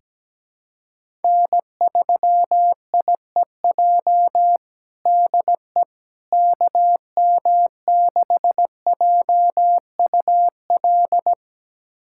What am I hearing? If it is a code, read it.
Morse code, heard as a keyed tone, about 17 wpm, 700 Hz. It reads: N3IEJ DE KM6JUL
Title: USA Callsign DE Callsign 17wpm